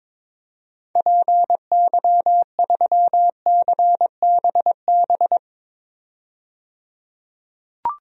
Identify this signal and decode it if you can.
Morse code, heard as a keyed tone, about 22 wpm, 700 Hz. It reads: PY3CBB
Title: DXCC-100 Call Signs - 6 Characters - 22wpm